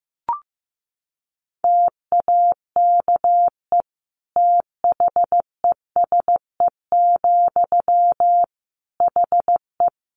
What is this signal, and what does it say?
Morse code, heard as a keyed tone, about 15 wpm, 700 Hz. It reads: TAKE THESE, HE
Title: Sentences from Top 300 Words 15wpm